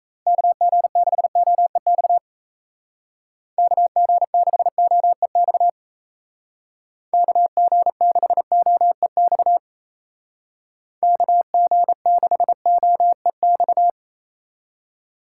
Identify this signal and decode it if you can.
Morse code, heard as a keyed tone, about 34 wpm, 700 Hz. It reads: KG6OEX KG6OEX KG6OEX KG6OEX
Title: Speed Racing - USA 2x3 Call Signs 28wpm